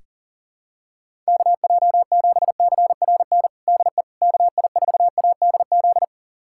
Morse code, heard as a keyed tone, about 40 wpm, 700 Hz. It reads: KJ7CRN DE KI4ADZ
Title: USA Callsign DE Callsign - 6 Characters - 40wpm